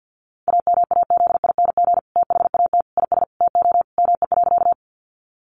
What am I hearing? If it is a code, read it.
Morse code, heard as a keyed tone, about 50 wpm, 700 Hz. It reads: AMAZING THAT IS TO ME.
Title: Sentences from Top 1500 Words 50wpm